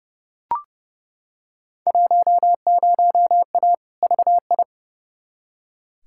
Morse code, heard as a keyed tone, about 30 wpm, 700 Hz. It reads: 10A VI
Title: Field Day Exchanges Encore - 30wpm